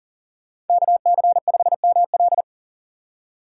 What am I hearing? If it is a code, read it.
Morse code, heard as a keyed tone, about 51 wpm, 700 Hz. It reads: KC5ML